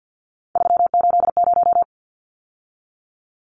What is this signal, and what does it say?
Morse code, heard as a keyed tone, about 50 wpm, 700 Hz. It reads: 380
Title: Numbers - 3-Digit 50wpm